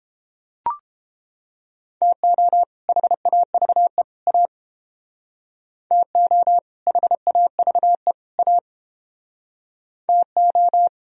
Morse code, heard as a keyed tone, about 29 wpm, 700 Hz. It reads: TO HAVE A TO HAVE A TO